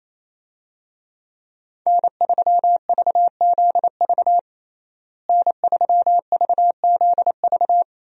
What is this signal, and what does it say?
Morse code, heard as a keyed tone, about 27 wpm, 700 Hz. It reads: N3VZV N3VZV
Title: USA Call Signs - 5 Characters - Repeated 2x - 28wpm